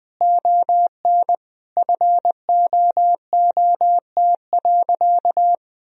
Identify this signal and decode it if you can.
Morse code, heard as a keyed tone, about 21 wpm, 700 Hz. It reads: ON FOOT.
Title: Sentences from Top 1000 Words - Review Part 3 - 20wpm